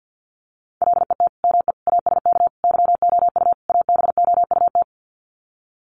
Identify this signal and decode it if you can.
Morse code, heard as a keyed tone, about 50 wpm, 700 Hz. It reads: LET ME ASK YOU ABOUT
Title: Sets of 5 Words 50wpm